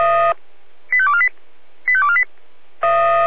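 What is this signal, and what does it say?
Five-tone selective call (ZVEI), heard as quick runs of five tones, tones of about 70 ms. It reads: 85248 85248
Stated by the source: ZVEI five-tone selective call received via SDR.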